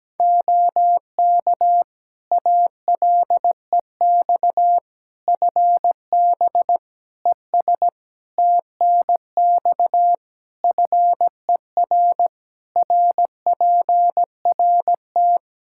Morse code, heard as a keyed tone, about 17 wpm, 700 Hz. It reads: OK ALEX FB ES TNX FER RPRT
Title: Practice QSO - Exchange 2 - Info - 17wpm